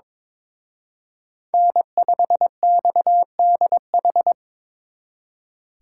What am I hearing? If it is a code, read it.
Morse code, heard as a keyed tone, about 26 wpm, 700 Hz. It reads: N5XDH
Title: Speed Racing - USA 1x3 Call Signs 22wpm